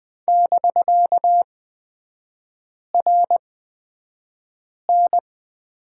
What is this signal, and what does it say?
Morse code, heard as a keyed tone, about 20 wpm, spaced as for 5 wpm, 700 Hz. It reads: <BK>RN
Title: 250 Lesson 40e - BK - Three Characters 20wpm